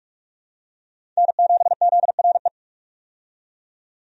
Morse code, heard as a keyed tone, about 46 wpm, 700 Hz. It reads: N7ZRE
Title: USA Call Signs - 5 Characters - 45wpm